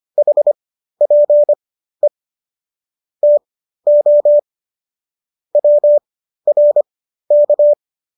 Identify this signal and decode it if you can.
Morse code, heard as a keyed tone, about 25 wpm, spaced as for 13 wpm, 590 Hz. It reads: HPE TO WRK